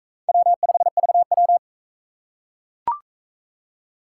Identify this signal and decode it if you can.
Morse code, heard as a keyed tone, about 39 wpm, 700 Hz. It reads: W5VW